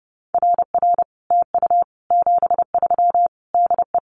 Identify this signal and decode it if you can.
Morse code, heard as a keyed tone, about 30 wpm, 700 Hz. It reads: RR TU 73 DE
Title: Common QSO Phrases Encore 30wpm